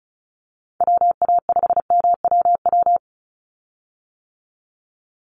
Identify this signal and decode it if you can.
Morse code, heard as a keyed tone, about 34 wpm, 700 Hz. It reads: WA5MWW
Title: USA Call Signs 35wpm